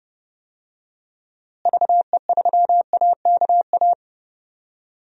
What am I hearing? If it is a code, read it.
Morse code, heard as a keyed tone, about 30 wpm, 700 Hz. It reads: VE3AKA
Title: DXCC-100 Call Signs - 6 Characters - Repeated 3x - 30wpm